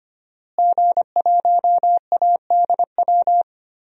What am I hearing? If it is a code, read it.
Morse code, heard as a keyed tone, about 25 wpm, 700 Hz. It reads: G1ADW
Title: DXCC-100 Call Signs - 5 Characters - Repeated 2x - 25wpm